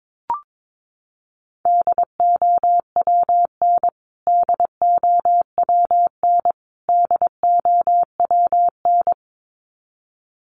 Morse code, heard as a keyed tone, about 22 wpm, 700 Hz. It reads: DOWN DOWN DOWN
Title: Top 200 Words Repeated 22wpm